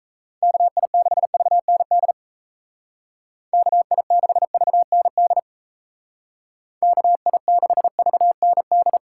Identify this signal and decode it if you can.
Morse code, heard as a keyed tone, about 34 wpm, 700 Hz. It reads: KI6VND KI6VND KI6VND
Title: Speed Racing - USA 2x3 Call Signs 28wpm